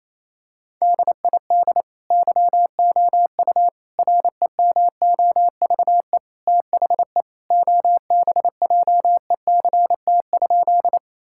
Morse code, heard as a keyed tone, about 27 wpm, 700 Hz. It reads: DID YOU REMOVE THE OBJECT?
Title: Sentences from Top 1000 Words - Review Part 5 - 28wpm